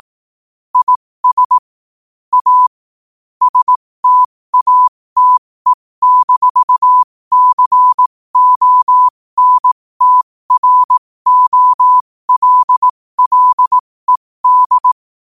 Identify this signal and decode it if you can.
Morse code, heard as a keyed tone, about 18 wpm, spaced as for 15 wpm, 1000 Hz. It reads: IS A STATE-CONTROLLED
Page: New quotes every day in morse code at 15 Words per minute.